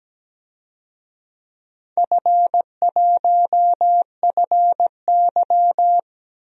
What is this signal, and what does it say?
Morse code, heard as a keyed tone, about 17 wpm, 700 Hz. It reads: F1FY
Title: DXCC-100 Call Signs - 4 Characters - Repeated 3x - 17wpm